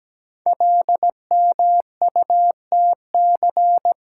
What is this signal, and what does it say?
Morse code, heard as a keyed tone, about 17 wpm, 700 Hz. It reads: LMUTC
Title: Code Groups - 5 Letters - 17wpm